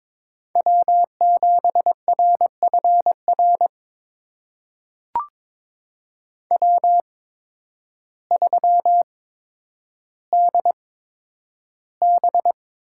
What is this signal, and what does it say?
Morse code, heard as a keyed tone, about 22 wpm, spaced as for 10 wpm, 700 Hz. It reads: W7RFR W3DB
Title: USA Call Signs - 8x Character Spacing - 22wpm